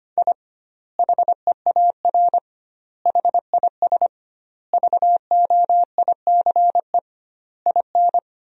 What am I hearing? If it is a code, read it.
Morse code, heard as a keyed tone, about 25 wpm, 700 Hz. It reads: I HEAR HIS VOICE IN
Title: 2x Word-Spacing - Sentences from Top 500 Words - 25wpm